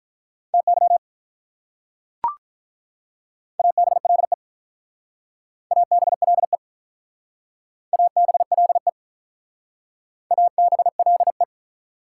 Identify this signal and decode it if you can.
Morse code, heard as a keyed tone, about 45 wpm, 700 Hz. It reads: TY ABLE ABLE ABLE ABLE